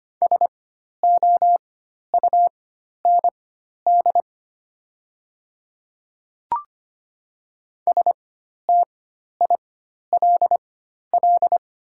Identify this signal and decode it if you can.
Morse code, heard as a keyed tone, about 25 wpm, spaced as for 12 wpm, 700 Hz. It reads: SOUND STILL
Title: 4x Character Spacing - Top 500 Words - 25wpm